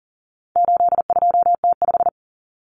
Morse code, heard as a keyed tone, about 40 wpm, 700 Hz. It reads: 82T5
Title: Common Cut Numbers - 4-Digit 40wpm